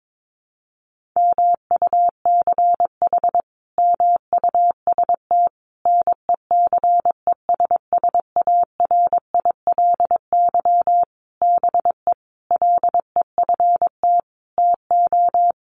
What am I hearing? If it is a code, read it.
Morse code, heard as a keyed tone, about 22 wpm, 700 Hz. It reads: MUCH MUST NECESSARILY BE LEFT TO
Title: Sentences from Top 1400 Words 22wpm